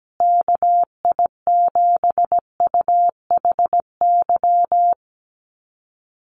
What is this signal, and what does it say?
Morse code, heard as a keyed tone, about 17 wpm, 700 Hz. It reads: KI7UHY
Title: USA Call Signs Repeated 2x 17wpm